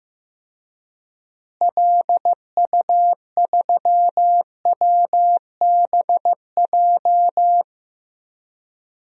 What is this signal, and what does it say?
Morse code, heard as a keyed tone, about 15 wpm, 700 Hz. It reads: LU3WBJ
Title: DXCC-100 Call Signs - 6 Characters - Repeated 3x - 15wpm